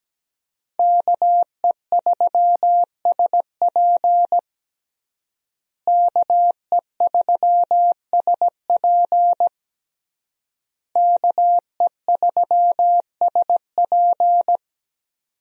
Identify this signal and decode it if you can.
Morse code, heard as a keyed tone, about 17 wpm, 700 Hz. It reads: KE3SP KE3SP KE3SP
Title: USA Call Signs - 5 Characters - Repeated 3x - 17wpm